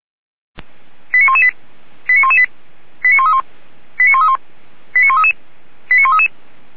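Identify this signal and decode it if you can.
Five-tone selective call (ZVEI), heard as quick runs of five tones, tones of about 70 ms. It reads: 89118 89118 89131 89131 89133 89133
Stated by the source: ZVEI five-tone selective call, received by SDR.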